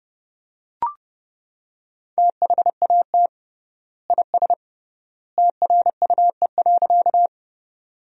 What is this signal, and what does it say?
Morse code, heard as a keyed tone, about 32 wpm, 700 Hz. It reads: THAT IS TRUE.
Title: 3x Word-Spacing - Sentences from Top 400 Words - 30wpm